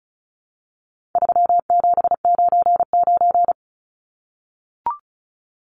Morse code, heard as a keyed tone, about 34 wpm, 700 Hz. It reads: 3799
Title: Numbers - 4-Digit 35wpm